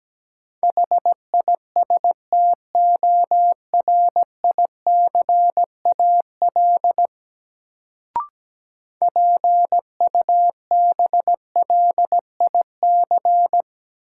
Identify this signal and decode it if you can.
Morse code, heard as a keyed tone, about 17 wpm, 700 Hz. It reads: HISTORICAL PUBLIC
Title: Sentences from Top 600 Words - New Word Drill - 17wpm